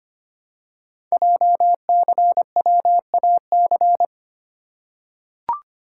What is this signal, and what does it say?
Morse code, heard as a keyed tone, about 25 wpm, 700 Hz. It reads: JCWAC
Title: Code Groups - 5 Letters - 25wpm